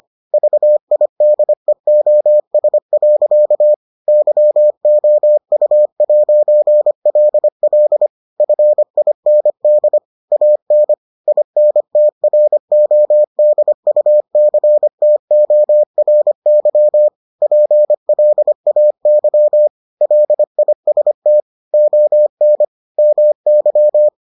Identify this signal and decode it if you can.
Morse code, heard as a keyed tone, about 25 wpm, 600 Hz. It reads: VIDEOS. YOU'LL FIND AN INTRODUCTORY PLAY LIST ON MY